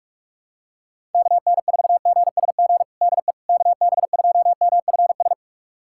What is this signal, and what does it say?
Morse code, heard as a keyed tone, about 46 wpm, 700 Hz. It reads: KN4GSG DE KB2MFS
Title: USA Callsign DE Callsign - 6 Characters - 45wpm